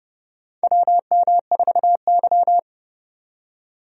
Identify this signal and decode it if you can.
Morse code, heard as a keyed tone, about 30 wpm, 700 Hz. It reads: WM4Y
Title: USA Call Signs - 4 Characters - 30wpm